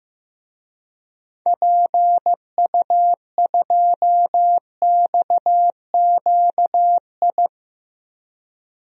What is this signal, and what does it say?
Morse code, heard as a keyed tone, about 15 wpm, 700 Hz. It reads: PU2XQI
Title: DXCC-100 Call Signs - 6 Characters - 15wpm